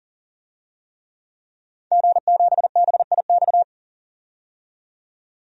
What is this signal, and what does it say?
Morse code, heard as a keyed tone, about 40 wpm, 700 Hz. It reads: G7BIX
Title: DXCC-100 Call Signs - 5 Characters - 40wpm